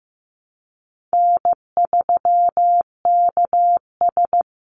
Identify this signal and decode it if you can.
Morse code, heard as a keyed tone, about 15 wpm, 700 Hz. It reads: N3KS
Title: Easy Call Signs Repeated 2x - 15wpm